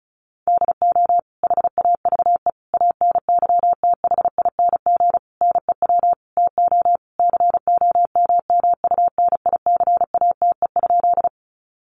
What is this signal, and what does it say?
Morse code, heard as a keyed tone, about 34 wpm, 700 Hz. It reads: DO HAVE ANYTHING NEW TO COMMUNICATE?
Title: Sentences from Top 1300 Words 35wpm